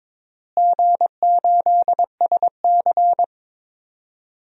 Morse code, heard as a keyed tone, about 21 wpm, 700 Hz. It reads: G8SC
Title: DXCC-100 Call Signs - 4 Characters - Repeated 3x - 22wpm